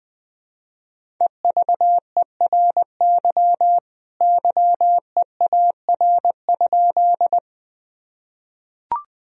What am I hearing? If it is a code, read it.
Morse code, heard as a keyed tone, about 20 wpm, 700 Hz. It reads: EVERY YEAR?
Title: Sentences from Top 1000 Words - Review Part 1 - 20wpm